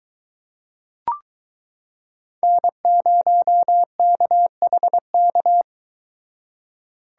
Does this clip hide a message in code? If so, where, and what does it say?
Yes: Morse code from 2.4 to 5.6 s, heard as a keyed tone, about 23 wpm, 700 Hz. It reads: N0KHK